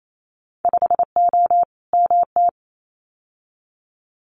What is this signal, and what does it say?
Morse code, heard as a keyed tone, about 28 wpm, 690 Hz. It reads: 5O MT
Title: Winter Field Day Exchanges Encore 28wpm